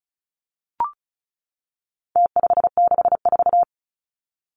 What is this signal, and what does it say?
Morse code, heard as a keyed tone, about 34 wpm, 700 Hz. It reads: T564
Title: Numbers - Contest Style - 4-Digit 35wpm